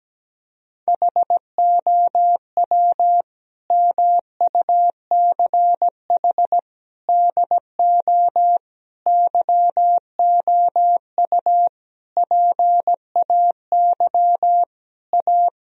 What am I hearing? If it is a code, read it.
Morse code, heard as a keyed tone, about 17 wpm, 700 Hz. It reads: HOW MUCH DO YOU PAY A
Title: Sentences from Top 1000 Words - Review Part 2 - 17wpm